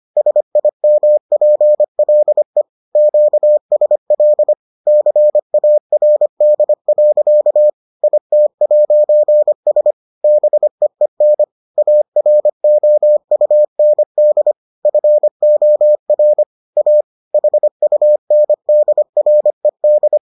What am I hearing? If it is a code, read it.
Morse code, heard as a keyed tone, about 25 wpm, 600 Hz. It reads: SIMPLE QSL CARD. IT'S BEEN AROUND FOR A HUNDRED